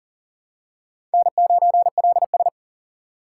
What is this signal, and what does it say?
Morse code, heard as a keyed tone, about 37 wpm, 700 Hz. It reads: N9LS
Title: USA Call Signs - 6x Character Spacing - 40wpm